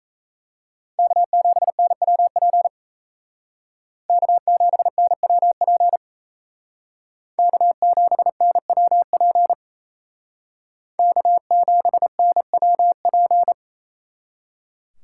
Morse code, heard as a keyed tone, about 35 wpm, 700 Hz. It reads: K7NWP K7NWP K7NWP K7NWP
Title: Speed Racing - USA 1x3 Call Signs 28wpm